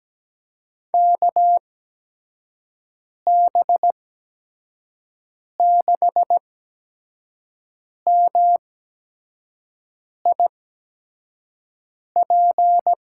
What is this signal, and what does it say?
Morse code, heard as a keyed tone, about 17 wpm, spaced as for 5 wpm, 700 Hz. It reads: KB6MIP
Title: USA Call Signs - 6 Characters - 8x Character Spacing - 17wpm